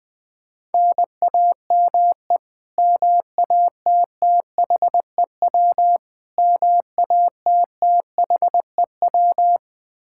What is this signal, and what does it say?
Morse code, heard as a keyed tone, about 21 wpm, 700 Hz. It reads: NAME MATTHEW MATTHEW
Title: K1USN SST QSO Practice 20wpm